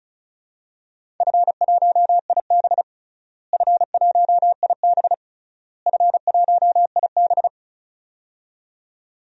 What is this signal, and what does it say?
Morse code, heard as a keyed tone, about 36 wpm, 700 Hz. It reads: F1IB F1IB F1IB
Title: DXCC-100 Call Signs - 4 Characters - Repeated 3x - 35wpm